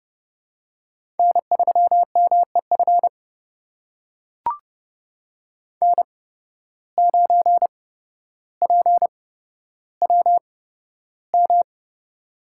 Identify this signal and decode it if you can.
Morse code, heard as a keyed tone, about 30 wpm, spaced as for 13 wpm, 700 Hz. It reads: N3MEF N9PWM
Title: USA Call Signs - 5 Characters - 8x Character Spacing - 30wpm